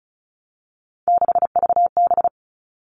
Morse code, heard as a keyed tone, about 34 wpm, 700 Hz. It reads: 6VB
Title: All Cut Numbers - 3-Digit 35wpm